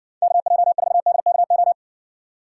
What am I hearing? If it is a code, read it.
Morse code, heard as a keyed tone, about 60 wpm, 700 Hz. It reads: KJ4DKO